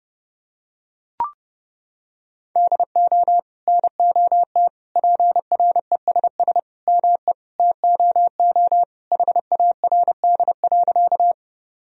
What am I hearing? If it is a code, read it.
Morse code, heard as a keyed tone, about 30 wpm, 700 Hz. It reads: DO NOT PRESS ME TOO HARD.
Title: Sentences from Top 1000 Words - Review Part 1 - 30wpm